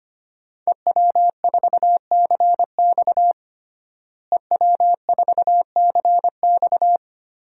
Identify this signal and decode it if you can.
Morse code, heard as a keyed tone, about 25 wpm, 700 Hz. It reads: EW4CX EW4CX
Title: DXCC-100 Call Signs - 5 Characters - Repeated 2x - 25wpm